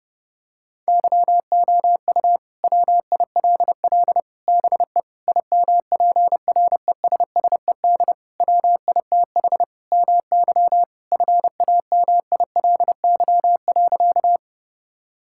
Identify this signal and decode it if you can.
Morse code, heard as a keyed tone, about 32 wpm, 700 Hz. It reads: YOU WILL BE IMPRESSED WITH MY FAMILY.
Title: Sentences from Top 1600 Words 30wpm